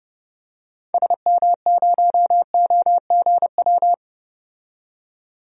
Morse code, heard as a keyed tone, about 30 wpm, 700 Hz. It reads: SM0OGW
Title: DXCC-100 Call Signs - 6 Characters - 30wpm